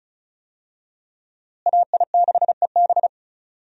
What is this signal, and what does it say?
Morse code, heard as a keyed tone, about 36 wpm, 700 Hz. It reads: AI6EB
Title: USA Call Signs - 5 Characters - Repeated 3x - 35wpm